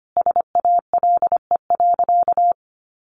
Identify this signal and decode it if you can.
Morse code, heard as a keyed tone, about 25 wpm, 700 Hz. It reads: SALE.
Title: Sentences from Top 1400 Words 25wpm